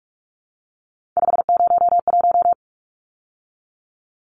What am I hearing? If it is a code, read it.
Morse code, heard as a keyed tone, about 44 wpm, 700 Hz. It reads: 501
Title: Numbers - 3-Digit 45wpm